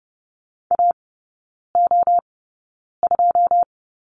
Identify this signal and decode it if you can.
Morse code, heard as a keyed tone, about 30 wpm, spaced as for 9 wpm, 700 Hz. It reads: AO2
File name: Lesson 19e - Letter w - Three Characters 30wpm